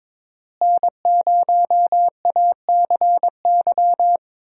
Morse code, heard as a keyed tone, about 22 wpm, 700 Hz. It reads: N0ACY
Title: USA Call Signs - 5 Characters - Repeated 2x - 22wpm